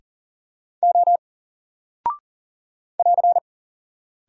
Morse code, heard as a keyed tone, about 41 wpm, 700 Hz. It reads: O <AR>
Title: Single Character (Letter:Number:Punctuation:Prosign) 40wpm